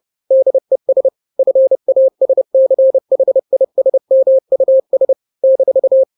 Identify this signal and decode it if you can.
Morse code, heard as a keyed tone, about 30 wpm, 530 Hz. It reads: DES FASCHISMUS -
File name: 29wpm